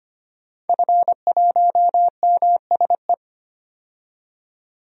Morse code, heard as a keyed tone, about 25 wpm, 700 Hz. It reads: F1MSE
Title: DXCC-100 Call Signs - 5 Characters - 25wpm